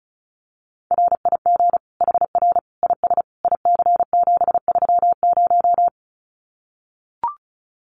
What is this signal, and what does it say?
Morse code, heard as a keyed tone, about 34 wpm, 700 Hz. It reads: RIG HR IS IC730
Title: Common QSO Phrases 35wpm